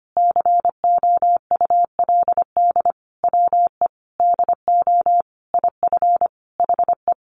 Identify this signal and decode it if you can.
Morse code, heard as a keyed tone, about 25 wpm, 700 Hz. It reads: COULD WE DO IF HE
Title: Sentences from Top 400 Words 25wpm